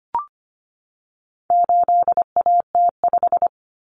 Morse code, heard as a keyed tone, about 25 wpm, 700 Hz. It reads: 8AT5
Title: Common Cut Numbers - 4-Digit 25wpm